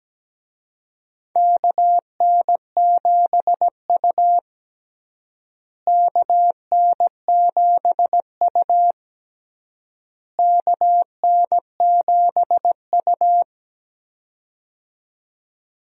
Morse code, heard as a keyed tone, about 17 wpm, 700 Hz. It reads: KN7U KN7U KN7U
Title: USA Call Signs - 4 Characters - Repeated 3x - 17wpm